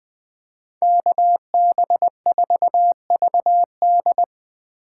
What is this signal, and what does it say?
Morse code, heard as a keyed tone, about 20 wpm, 700 Hz. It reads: KB4VD
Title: Speed Racing - USA 2x2 Call Signs 17wpm